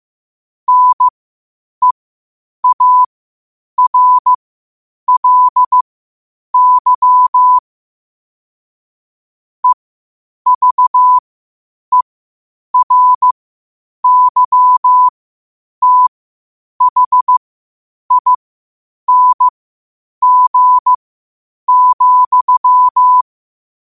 Morse code, heard as a keyed tone, about 15 wpm, spaced as for 8 wpm, 1000 Hz. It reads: NEARLY EVERYTHING,
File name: qotdfiles / 8wpm